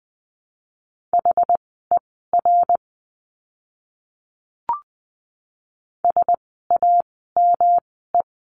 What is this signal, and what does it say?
Morse code, heard as a keyed tone, about 20 wpm, spaced as for 14 wpm, 700 Hz. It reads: HER SAME
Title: 230 Lesson 37f - Letter x - Words 20wpm